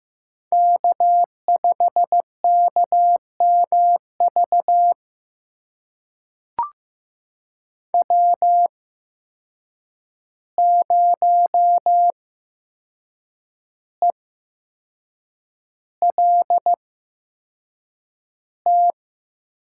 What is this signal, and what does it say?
Morse code, heard as a keyed tone, about 15 wpm, spaced as for 6 wpm, 700 Hz. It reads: K5KMV W0ELT
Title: USA Call Signs - 5 Characters - 8x Character Spacing - 15wpm